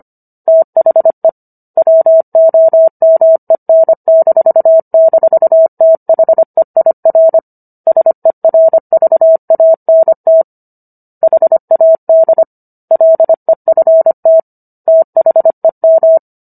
Morse code, heard as a keyed tone, about 25 wpm, 650 Hz. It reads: THE WOMEN--THEIR SERVANT HAD LEFT THEM